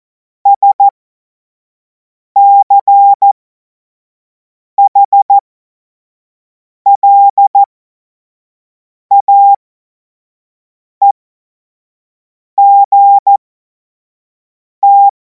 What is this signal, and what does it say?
Morse code, heard as a keyed tone, about 14 wpm, spaced as for 5 wpm, 800 Hz. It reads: SCHLAEGT
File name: Morsekurs / Witze / Witz 17. Februar 2026 25 F70